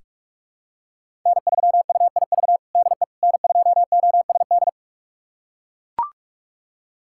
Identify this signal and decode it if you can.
Morse code, heard as a keyed tone, about 41 wpm, 700 Hz. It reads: N3UIV DE N2OSD
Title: USA Callsign DE Callsign - 5 Characters - 45wpm